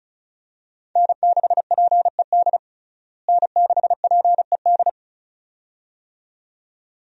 Morse code, heard as a keyed tone, about 36 wpm, 700 Hz. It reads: N6PED N6PED
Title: USA Call Signs - 5 Characters - Repeated 2x - 35wpm